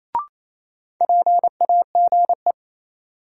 Morse code, heard as a keyed tone, about 28 wpm, 700 Hz. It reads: PAGE
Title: Sentences from Top 1000 Words - Review Part 1 - New Word Drill - 28wpm